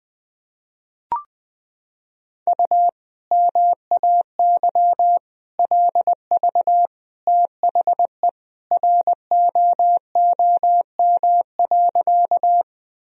Morse code, heard as a keyed tone, about 20 wpm, 700 Hz. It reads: U MAY LV THE ROOM.
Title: Sentences from Top 400 Words and Abbreviations 20wpm